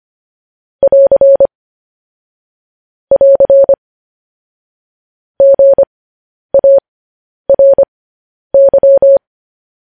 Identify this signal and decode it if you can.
Morse code, heard as a keyed tone, about 25 wpm, spaced as for 10 wpm, 550 Hz. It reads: <AR> <AR> GARY
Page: News Headlines in Morse Code at 10 WPM